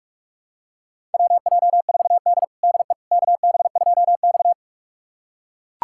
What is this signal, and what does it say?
Morse code, heard as a keyed tone, about 42 wpm, 700 Hz. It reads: WJ4D DE KB2X
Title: USA Callsign DE Callsign - 4 Characters - 45wpm